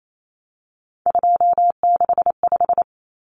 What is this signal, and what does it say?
Morse code, heard as a keyed tone, about 28 wpm, 700 Hz. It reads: 265
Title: Numbers - 3-Digit 28wpm